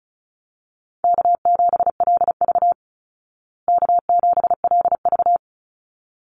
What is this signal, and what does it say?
Morse code, heard as a keyed tone, about 34 wpm, 700 Hz. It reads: K7LV K7LV
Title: Easy Call Signs Repeated 2x - 35wpm